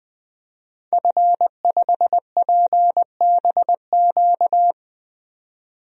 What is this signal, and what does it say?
Morse code, heard as a keyed tone, about 21 wpm, 700 Hz. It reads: F5PBQ
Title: DXCC-100 Call Signs - 5 Characters - 20wpm